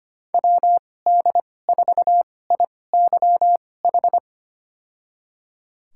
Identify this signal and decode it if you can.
Morse code, heard as a keyed tone, about 25 wpm, spaced as for 18 wpm, 700 Hz. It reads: WD4IYH
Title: USA Call Signs - 6 Characters - 2x Character Spacing - 25wpm